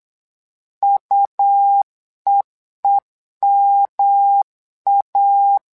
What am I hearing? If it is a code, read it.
Morse code, heard as a keyed tone, about 8 wpm, 800 Hz. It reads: UEEMA
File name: LEZIONI / 3 MWRO / 3 MWRO 8wpm 2606